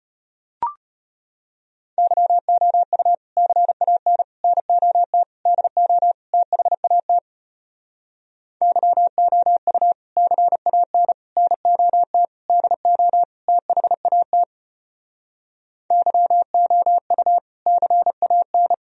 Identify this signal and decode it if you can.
Morse code, heard as a keyed tone, about 32 wpm, 700 Hz. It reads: YOU CAN NOT DO THAT YOU CAN NOT DO THAT YOU CAN